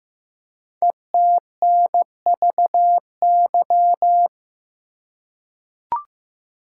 Morse code, heard as a keyed tone, about 15 wpm, 700 Hz. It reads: ETNVY
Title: Code Groups - 5 Letters - 15wpm